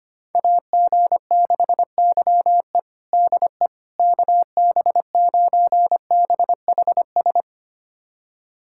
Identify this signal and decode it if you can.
Morse code, heard as a keyed tone, about 25 wpm, 700 Hz. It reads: AG6YE DE KB9BHS
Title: USA Callsign DE Callsign 25wpm